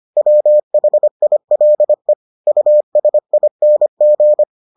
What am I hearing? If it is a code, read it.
Morse code, heard as a keyed tone, about 25 wpm, 600 Hz. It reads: WHILE USING